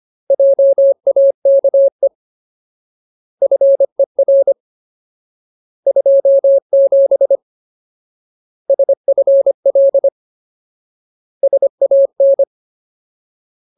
Morse code, heard as a keyed tone, about 25 wpm, 550 Hz. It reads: JAKE FER 27 SFL SAN
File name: Ultimate Word List / 25WPM Ultimate